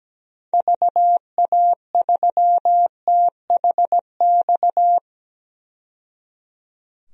Morse code, heard as a keyed tone, about 17 wpm, 700 Hz. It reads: VA3THX
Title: DXCC-100 Call Signs - 6 Characters - 17wpm